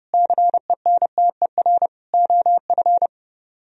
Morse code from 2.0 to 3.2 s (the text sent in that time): OF